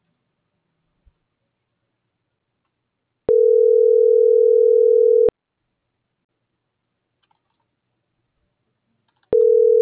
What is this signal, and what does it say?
Ringback tone (North America), 440 + 480 Hz.
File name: CENTRAL MABE / out manuel.posadas